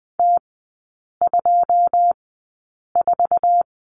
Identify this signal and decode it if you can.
Morse code, heard as a keyed tone, about 20 wpm, spaced as for 8 wpm, 700 Hz. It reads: T24
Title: 114 Lesson 19e - Letter w - Three Characters 20wpm